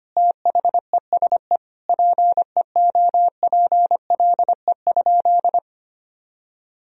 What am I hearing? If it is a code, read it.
Morse code, heard as a keyed tone, about 25 wpm, 700 Hz. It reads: THESE PEOPLE?
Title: Sentences from Top 1000 Words - Review Part 3 - 25wpm